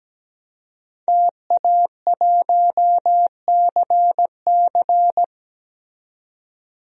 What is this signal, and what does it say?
Morse code, heard as a keyed tone, about 17 wpm, 700 Hz. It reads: TA1CC
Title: DXCC-100 Call Signs - 5 Characters - 17wpm